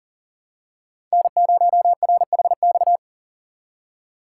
Morse code, heard as a keyed tone, about 40 wpm, 700 Hz. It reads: N0RHX
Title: USA Call Signs - 5 Characters - 2x Character Spacing - 40wpm